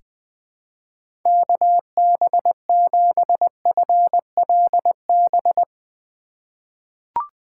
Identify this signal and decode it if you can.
Morse code, heard as a keyed tone, about 20 wpm, 700 Hz. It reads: KB7FLB
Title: USA Call Signs - 6 Characters - 2x Character Spacing - 20wpm